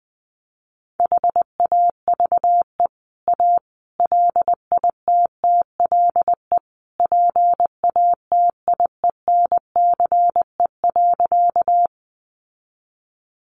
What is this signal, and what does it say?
Morse code, heard as a keyed tone, about 20 wpm, 700 Hz. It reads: HAVE A LITTLE PATIENCE.
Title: Sentences from Top 1300 Words 20wpm